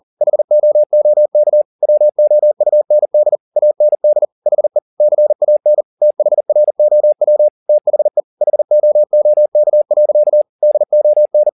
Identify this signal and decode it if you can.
Morse code, heard as a keyed tone, about 40 wpm, 600 Hz. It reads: HOOK WOUND AND HE CAN THROW THE HOOK. DON
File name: the old man and the sea 2 40wpm